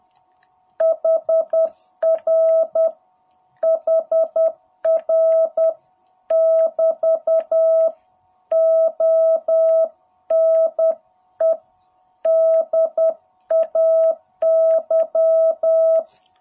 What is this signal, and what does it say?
Morse code, heard as a keyed tone, about 10 wpm, 660 Hz. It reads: HR HR <BT> ONE DAY
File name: jh8cbh / 2021cw